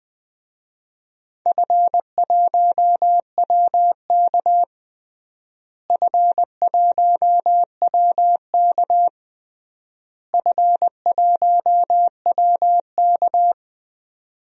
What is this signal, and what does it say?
Morse code, heard as a keyed tone, about 20 wpm, 700 Hz. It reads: F1WK F1WK F1WK
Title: DXCC-100 Call Signs - 4 Characters - Repeated 3x - 20wpm